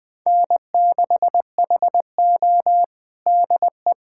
Morse code, heard as a keyed tone, about 20 wpm, 700 Hz. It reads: N6HO DE
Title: USA Callsign DE Callsign - 4 Characters - 20wpm